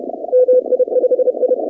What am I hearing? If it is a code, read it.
Morse code, heard as a keyed tone, about 30 wpm, 500 Hz. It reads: MI5I
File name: contests / 2019-cw-wpx-kt5j-k5pi- -2330 0-26